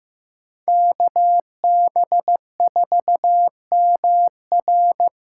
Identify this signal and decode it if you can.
Morse code, heard as a keyed tone, about 16 wpm, 700 Hz. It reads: KB4MR
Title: USA Callsign DE Callsign - 5 Characters - 15wpm